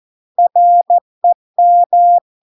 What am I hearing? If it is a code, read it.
Morse code, heard as a keyed tone, about 14 wpm, 700 Hz. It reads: REM